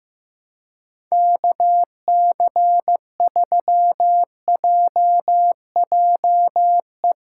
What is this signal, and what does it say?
Morse code, heard as a keyed tone, about 15 wpm, 700 Hz. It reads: KC3JJE
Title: USA Callsign DE Callsign 15wpm